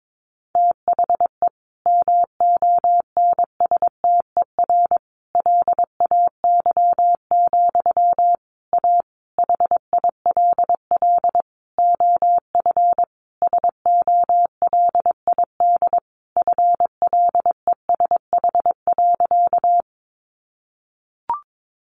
Morse code, heard as a keyed tone, about 22 wpm, 700 Hz. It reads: THE MONSTER LAY, A HILL OF SOLID FLESH.
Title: A Sound Of Thunder 22wpm